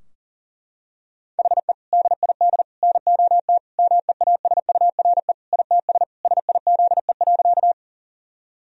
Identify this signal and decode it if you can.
Morse code, heard as a keyed tone, about 40 wpm, 700 Hz. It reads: HE DID NOT MEASURE ITS SIZE.
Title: Sentences from Top 1000 Words - Review Part 2 - 40wpm